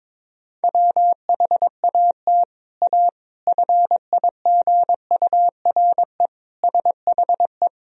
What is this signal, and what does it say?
Morse code, heard as a keyed tone, about 21 wpm, 700 Hz. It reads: WHAT A FIGURE SHE
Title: Sentences from Top 1000 Words - Review Part 2 - 22wpm